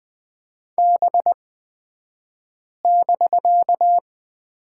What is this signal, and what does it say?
Morse code, heard as a keyed tone, about 20 wpm, spaced as for 5 wpm, 700 Hz. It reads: B<BK>
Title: 249 Lesson 40d - BK - Two Characters 20wpm